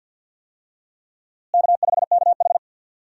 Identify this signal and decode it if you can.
Morse code, heard as a keyed tone, about 52 wpm, 700 Hz. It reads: K5KH
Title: USA Call Signs - 4 Characters - Repeated 2x - 50wpm